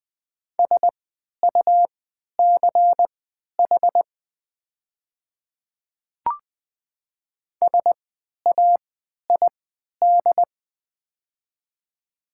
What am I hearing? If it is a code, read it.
Morse code, heard as a keyed tone, about 20 wpm, spaced as for 11 wpm, 700 Hz. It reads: SUCH SAID
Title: 3x Character Spacing - Top 300 Words - 20wpm